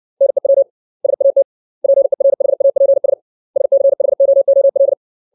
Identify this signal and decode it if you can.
Morse code, heard as a keyed tone, about 60 wpm, 550 Hz. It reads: NEW SAT PERHAPS SCHOOL
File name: 632 words / 632 Words 60WPM